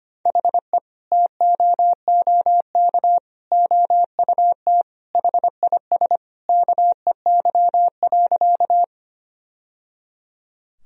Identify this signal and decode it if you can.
Morse code, heard as a keyed tone, about 25 wpm, 700 Hz. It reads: HE TOOK OUT HIS KEY.
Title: Sentences from Top 1000 Words - Review Part 4 - 25wpm